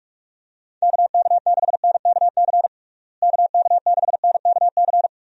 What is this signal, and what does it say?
Morse code, heard as a keyed tone, about 41 wpm, 700 Hz. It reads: KK6NKC KK6NKC
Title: USA Call Signs - 6 Characters - Repeated 3x - 45wpm